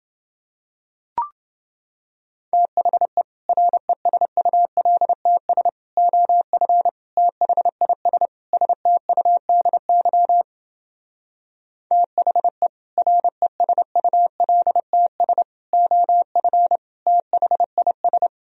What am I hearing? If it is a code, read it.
Morse code, heard as a keyed tone, about 29 wpm, 700 Hz. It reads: THE RESULTS OF THIS STUDY THE RESULTS OF THIS